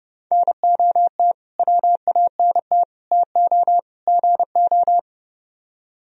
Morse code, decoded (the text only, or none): NOT WANT TO GO